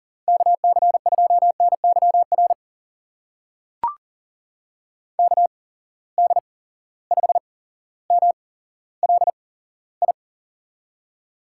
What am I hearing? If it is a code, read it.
Morse code, heard as a keyed tone, about 40 wpm, spaced as for 17 wpm, 700 Hz. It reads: KC2NYR KD5MLI
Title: USA Call Signs - 6 Characters - 8x Character Spacing - 40wpm